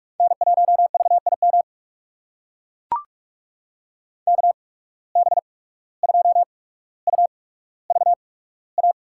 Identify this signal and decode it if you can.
Morse code, heard as a keyed tone, about 44 wpm, spaced as for 18 wpm, 700 Hz. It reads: N1VIM KB2UVA
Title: USA Call Signs - 8x Character Spacing - 45wpm